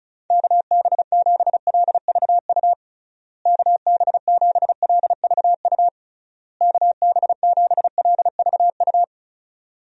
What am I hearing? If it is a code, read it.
Morse code, heard as a keyed tone, about 34 wpm, 700 Hz. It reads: KB7LVU KB7LVU KB7LVU
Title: USA Call Signs - 6 Characters - Repeated 3x - 35wpm